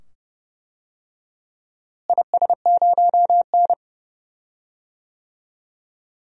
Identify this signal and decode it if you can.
Morse code, heard as a keyed tone, about 32 wpm, 700 Hz. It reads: IS0N
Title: DXCC-100 Call Signs - 4 Characters - Repeated 2x - 30wpm